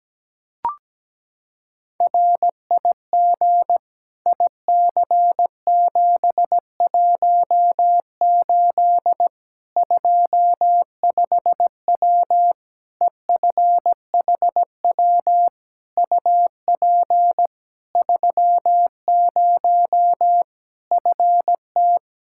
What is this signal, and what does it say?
Morse code, heard as a keyed tone, about 17 wpm, 700 Hz. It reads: RIG IC718 25W EFHW UP 30 FT
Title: Practice QSO - Exchange 2 - Info - 17wpm